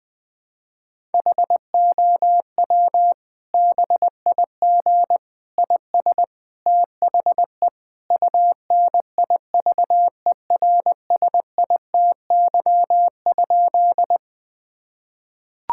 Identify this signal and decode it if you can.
Morse code, heard as a keyed tone, about 20 wpm, 700 Hz. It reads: HOW BIG IS THE UNIVERSITY?
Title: Sentences from Top 1000 Words - Review Part 5 - 20wpm